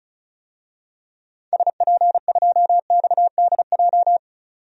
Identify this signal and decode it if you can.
Morse code, heard as a keyed tone, about 34 wpm, 700 Hz. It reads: SP2XDJ
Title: DXCC-100 Call Signs - 6 Characters - 35wpm